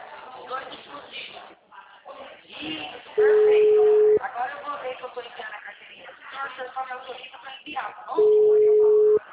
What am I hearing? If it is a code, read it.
Ringback tone (Europe, 425 Hz), with speech.
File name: out- -2001- - - .7436